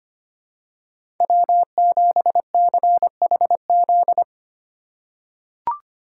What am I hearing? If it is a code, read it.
Morse code, heard as a keyed tone, about 25 wpm, 700 Hz. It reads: W7CHZ
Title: USA Call Signs - 5 Characters - 4x Character Spacing - 25wpm